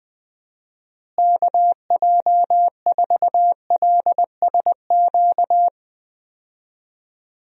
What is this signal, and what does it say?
Morse code, heard as a keyed tone, about 20 wpm, 700 Hz. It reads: KJ4LSQ
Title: USA Call Signs - 6 Characters - Repeated 2x - 20wpm